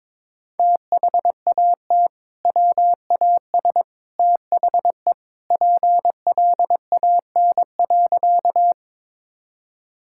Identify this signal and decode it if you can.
Morse code, heard as a keyed tone, about 22 wpm, 700 Hz. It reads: THAT WAS THE PLAN.
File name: practice / 22wpm / Sentences with 2-Word Phrases - 22wpm